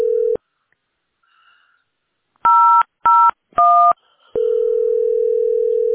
DTMF, heard as telephone touch tones, tones of about 370 ms. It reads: 001